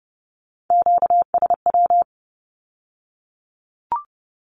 Morse code, heard as a keyed tone, about 30 wpm, 700 Hz. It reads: QSW
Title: Q-Codes with Spoken Meaning - 30wpm